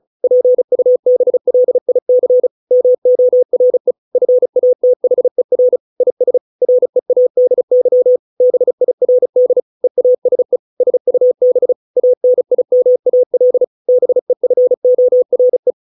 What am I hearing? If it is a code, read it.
Morse code, heard as a keyed tone, about 34 wpm, 500 Hz. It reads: PUBLIC MORE FATHER IS READY BIRD EASE SUB ANIMAL BEFORE
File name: 632 words / 632 Words 35WPM